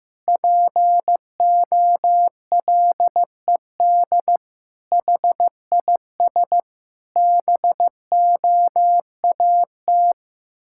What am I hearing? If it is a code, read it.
Morse code, heard as a keyed tone, about 16 wpm, 700 Hz. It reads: POLED HIS BOAT
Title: Pangram Challenge v2 - 15wpm